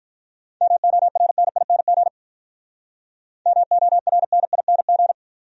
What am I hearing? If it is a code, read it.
Morse code, heard as a keyed tone, about 45 wpm, 700 Hz. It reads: MORNING MORNING